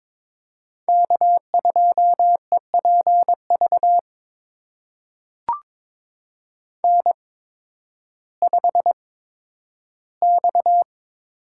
Morse code, heard as a keyed tone, about 22 wpm, spaced as for 12 wpm, 700 Hz. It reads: K2EPV N5X
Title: USA Call Signs - 8x Character Spacing - 22wpm